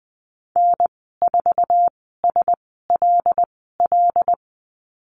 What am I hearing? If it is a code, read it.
Morse code, heard as a keyed tone, about 20 wpm, spaced as for 14 wpm, 700 Hz. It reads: N4SLL
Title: 076 Lesson 13g - Letter L - Call Signs 20wpm